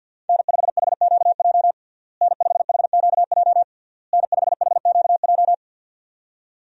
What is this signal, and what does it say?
Morse code, heard as a keyed tone, about 50 wpm, 700 Hz. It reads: N5HQJ N5HQJ N5HQJ
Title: USA Call Signs - 5 Characters - Repeated 3x - 50wpm